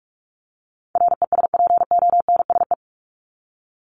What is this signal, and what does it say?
Morse code, heard as a keyed tone, about 44 wpm, 700 Hz. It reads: RESPONSE
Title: Top 1251-1500 Words 45wpm